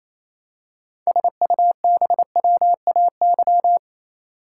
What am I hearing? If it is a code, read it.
Morse code, heard as a keyed tone, about 27 wpm, 690 Hz. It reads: SUBWAY
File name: Easy Words - Speed-Racing with Hints - 22wpm